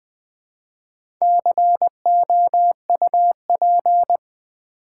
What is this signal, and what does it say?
Morse code, heard as a keyed tone, about 20 wpm, 700 Hz. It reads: COUP
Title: Build A Word - 2 Letters At A Time - 4-Letter Words - 20wpm